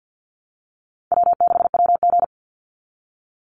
Morse code, heard as a keyed tone, about 48 wpm, 700 Hz. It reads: W6WG
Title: Easy Call Signs - 50wpm